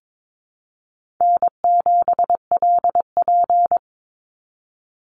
Morse code, heard as a keyed tone, about 22 wpm, 700 Hz. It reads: N7LP
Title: NA SOTA Chaser Callsigns 22wpm